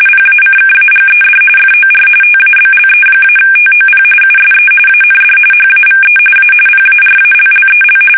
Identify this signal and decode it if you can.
RTTY, about 45.45 baud. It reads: PACKAGED FOR TABLE-TOP USE AND HAVE A PARALLEL
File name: fsk850 / teletype-corp